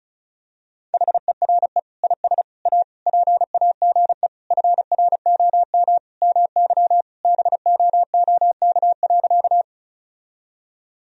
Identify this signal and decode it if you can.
Morse code, heard as a keyed tone, about 34 wpm, 700 Hz. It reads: HERE IS A PAGE FROM MY BOOK.
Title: Sentences from Top 1000 Words - Review Part 1 - 35wpm